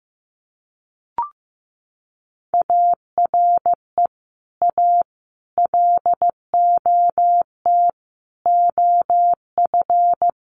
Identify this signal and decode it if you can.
Morse code, heard as a keyed tone, about 15 wpm, 700 Hz. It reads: ARE A LOT OF
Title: Sets of 5 Words Encore2 15wpm